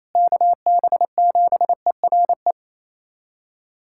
Morse code, heard as a keyed tone, about 28 wpm, 700 Hz. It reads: KB7ERE
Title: USA Callsign DE Callsign - 6 Characters - 28wpm